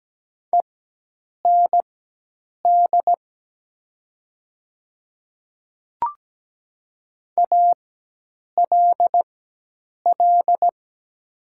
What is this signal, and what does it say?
Morse code, heard as a keyed tone, about 17 wpm, spaced as for 8 wpm, 700 Hz. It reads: END ALL
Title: 4x Character Spacing - Top 300 Words - 17wpm